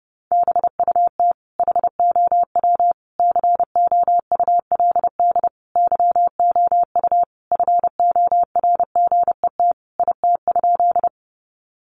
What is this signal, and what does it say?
Morse code, heard as a keyed tone, about 30 wpm, 700 Hz. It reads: BUT HOW COULD YOU FORGET IT?
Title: Sentences from Top 1200 Words 30wpm